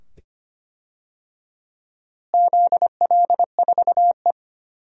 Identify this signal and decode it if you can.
Morse code, heard as a keyed tone, about 25 wpm, 700 Hz. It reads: ZL4E
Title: DXCC-100 Call Signs - 4 Characters - Repeated 2x - 25wpm